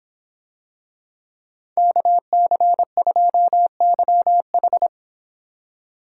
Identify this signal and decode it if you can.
Morse code, heard as a keyed tone, about 26 wpm, 700 Hz. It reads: KC2YH